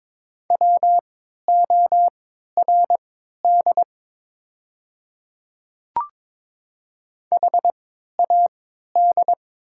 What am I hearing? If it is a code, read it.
Morse code, heard as a keyed tone, about 22 wpm, spaced as for 13 wpm, 690 Hz. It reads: WORD HAD
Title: 3x Character Spacing - Top 200 Words - 22wpm